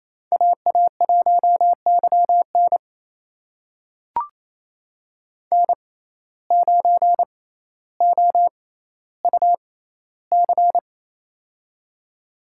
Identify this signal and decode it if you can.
Morse code, heard as a keyed tone, about 28 wpm, spaced as for 14 wpm, 700 Hz. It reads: AA1YN N9OUC
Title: USA Call Signs - 5 Characters - 6x Character Spacing - 28wpm